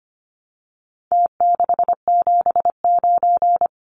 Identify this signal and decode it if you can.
Morse code, heard as a keyed tone, about 25 wpm, 700 Hz. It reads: T679
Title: Numbers - Contest Style - 4-Digit 25wpm